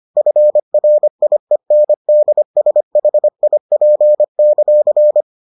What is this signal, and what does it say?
Morse code, heard as a keyed tone, about 25 wpm, 600 Hz. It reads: FRIENDSHIP;